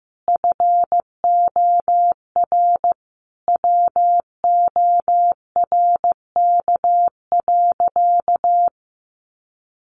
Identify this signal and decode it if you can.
Morse code, heard as a keyed tone, about 15 wpm, 700 Hz. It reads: FOR WORK.
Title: Sentences from Top 1400 Words 15wpm